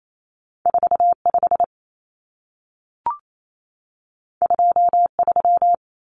Morse code, heard as a keyed tone, about 28 wpm, 700 Hz. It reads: 45 23
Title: Numbers - 2-Digit 28wpm